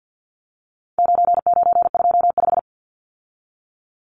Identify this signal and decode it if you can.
Morse code, heard as a keyed tone, about 50 wpm, 700 Hz. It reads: 9925
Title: Numbers - 4-Digit 50wpm